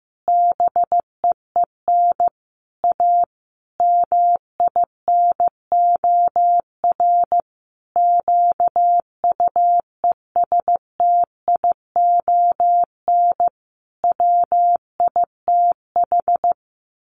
Morse code, heard as a keyed tone, about 15 wpm, 700 Hz. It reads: BEEN A MINOR QUESTION WITH
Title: Sentences from Top 1500 Words 15wpm